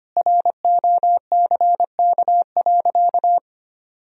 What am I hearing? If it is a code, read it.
Morse code, heard as a keyed tone, about 25 wpm, 700 Hz. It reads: ROCK.
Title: Sentences from Top 1000 Words - Review Part 2 - 25wpm